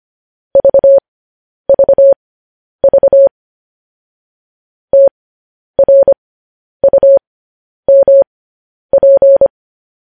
Morse code, heard as a keyed tone, about 25 wpm, spaced as for 10 wpm, 550 Hz. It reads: VVV TRUMP
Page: news headlines in morse code at 10 wpm